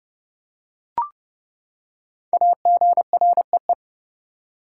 Morse code, heard as a keyed tone, about 29 wpm, 700 Hz. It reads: AGREE
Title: Five Letter Words - 30wpm